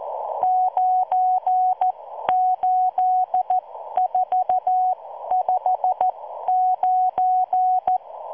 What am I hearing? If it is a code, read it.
Morse code, heard as a keyed tone, about 14 wpm, spaced as for 12 wpm, 740 Hz. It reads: 98459